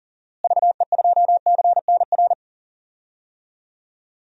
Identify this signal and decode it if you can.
Morse code, heard as a keyed tone, about 40 wpm, 700 Hz. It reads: VE2CNR
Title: DXCC-100 Call Signs - 6 Characters - Repeated 3x - 40wpm